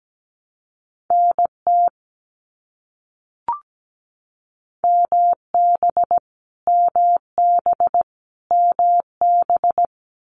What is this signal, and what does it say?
Morse code, heard as a keyed tone, about 17 wpm, 700 Hz. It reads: NT MB MB MB
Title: Canadian Provinces and Territories - Repeated 3x - 17wpm